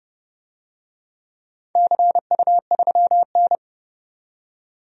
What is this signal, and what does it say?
Morse code, heard as a keyed tone, about 30 wpm, 700 Hz. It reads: CU3N
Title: DXCC-100 Call Signs - 4 Characters - 30wpm